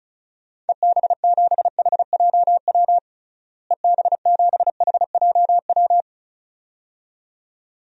Morse code, heard as a keyed tone, about 36 wpm, 700 Hz. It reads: EB7HJW EB7HJW
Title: DXCC-100 Call Signs - 6 Characters - Repeated 2x - 35wpm